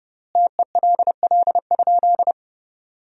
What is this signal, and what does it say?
Morse code, heard as a keyed tone, about 32 wpm, 700 Hz. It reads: TELL?
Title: 4x Word-Spacing - Sentences from Top 300 Words - 30wpm